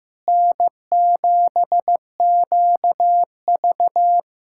Morse code, heard as a keyed tone, about 16 wpm, 700 Hz. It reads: N7QV
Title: USA Callsign DE Callsign - 4 Characters - 15wpm